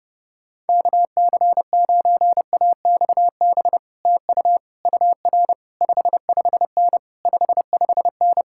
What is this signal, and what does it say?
Morse code, heard as a keyed tone, about 33 wpm, 700 Hz. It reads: KC9AXB TU UR 55N 55N